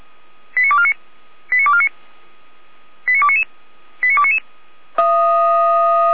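Five-tone selective call (ZVEI), heard as quick runs of five tones, tones of about 80 ms. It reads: 89249 89249 89299 89299
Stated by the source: ZVEI five-tone selective call 89310 received by SDR.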